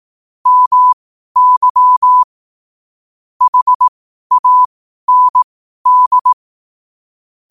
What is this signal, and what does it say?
Morse code, heard as a keyed tone, about 18 wpm, spaced as for 12 wpm, 1000 Hz. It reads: MY HAND
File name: qotdfiles / 12wpm